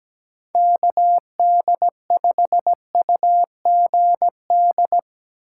Morse code, heard as a keyed tone, about 17 wpm, 700 Hz. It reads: KD5UGD
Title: USA Call Signs - 6 Characters - 3x Character Spacing - 17wpm